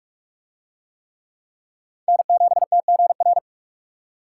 Morse code, heard as a keyed tone, about 44 wpm, 700 Hz. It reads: N7TGR